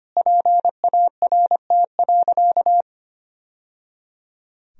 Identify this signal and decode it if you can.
Morse code, heard as a keyed tone, about 25 wpm, 700 Hz. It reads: PART.
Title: Sentences from Top 1000 Words - Review Part 1 - 25wpm